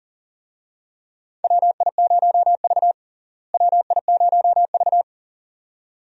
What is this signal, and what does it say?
Morse code, heard as a keyed tone, about 40 wpm, 700 Hz. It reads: WI0V WI0V
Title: USA Call Signs - 4 Characters - Repeated 2x - 40wpm